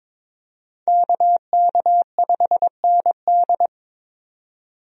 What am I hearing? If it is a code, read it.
Morse code, heard as a keyed tone, about 22 wpm, 700 Hz. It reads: KK5ND
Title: USA Call Signs - 5 Characters - 22wpm